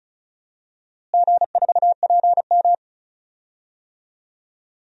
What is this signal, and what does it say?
Morse code, heard as a keyed tone, about 34 wpm, 700 Hz. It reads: G4PM
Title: DXCC-100 Call Signs - 4 Characters - Repeated 3x - 35wpm